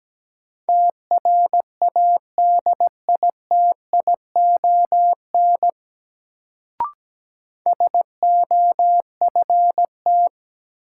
Morse code, heard as a keyed tone, about 17 wpm, 700 Hz. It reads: TRADITION SOFT
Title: Sentences from Top 1500 Words - New Word Drill Alt - 17wpm